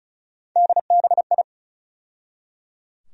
Morse code, heard as a keyed tone, about 34 wpm, 700 Hz. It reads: DBI
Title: USA Call Sign Postfixes - 3 Letters 35wpm